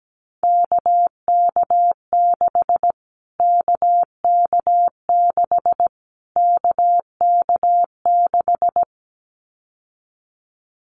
Morse code, heard as a keyed tone, about 17 wpm, 700 Hz. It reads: KK6 KK6 KK6
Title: Callsign Prefixes for Top 25 DXCC - Repeated 3x - 17wpm